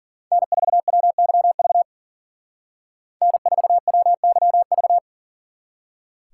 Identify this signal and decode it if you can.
Morse code, heard as a keyed tone, about 48 wpm, 700 Hz. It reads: N4WYV N4WYV
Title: Speed Racing - USA Call Signs 40wpm